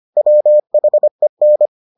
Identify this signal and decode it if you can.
Morse code, heard as a keyed tone, about 25 wpm, 600 Hz. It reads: WHEN